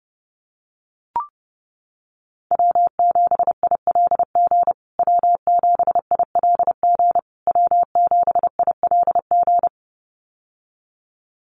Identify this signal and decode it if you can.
Morse code, heard as a keyed tone, about 30 wpm, 700 Hz. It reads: W7ILG W7ILG W7ILG
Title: USA Call Signs Repeated 3x 30wpm